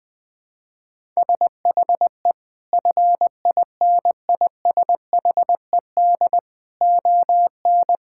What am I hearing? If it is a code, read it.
Morse code, heard as a keyed tone, about 20 wpm, 700 Hz. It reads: SHE FINISHED ON
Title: Sentences with 8-Word Phrases - 20wpm